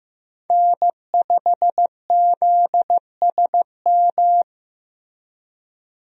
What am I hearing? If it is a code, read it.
Morse code, heard as a keyed tone, about 15 wpm, 700 Hz. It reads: N5ZSM
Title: USA Call Signs - 5 Characters - Repeated 3x - 15wpm